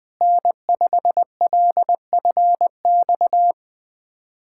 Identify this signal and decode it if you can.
Morse code, heard as a keyed tone, about 20 wpm, 700 Hz. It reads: N5LFX
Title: USA Callsign DE Callsign 20wpm